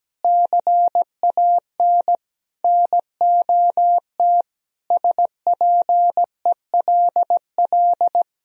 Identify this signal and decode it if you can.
Morse code, heard as a keyed tone, about 17 wpm, 700 Hz. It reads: CAN NOT SPELL
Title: Sentences from Top 1000 Words - Review Part 1 - 17wpm